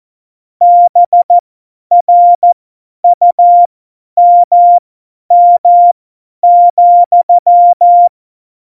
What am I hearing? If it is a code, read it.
Morse code, heard as a keyed tone, about 14 wpm, spaced as for 10 wpm, 700 Hz. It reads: BRUMM,
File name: Morsekurs / Witze / Witz 23. Januar 2018, 50 F70